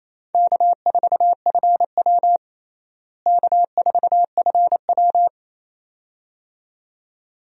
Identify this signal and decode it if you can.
Morse code, heard as a keyed tone, about 28 wpm, 700 Hz. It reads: K4FW K4FW
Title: USA Call Signs - 4 Characters - Repeated 2x - 28wpm